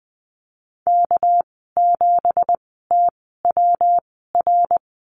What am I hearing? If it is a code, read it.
Morse code, heard as a keyed tone, about 20 wpm, spaced as for 14 wpm, 700 Hz. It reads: K7TWR
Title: 231 Lesson 37g - Letter x - Call Signs 20wpm